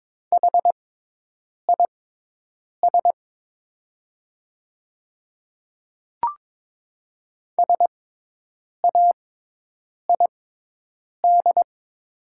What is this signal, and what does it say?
Morse code, heard as a keyed tone, about 22 wpm, spaced as for 8 wpm, 690 Hz. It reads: HIS SAID
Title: 6x Character Spacing - Top 100 Words - 22wpm